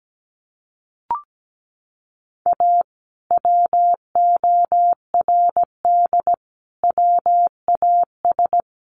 Morse code, heard as a keyed tone, about 17 wpm, 700 Hz. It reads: A WORD WAS
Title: Sentences from Top 1400 Words 17wpm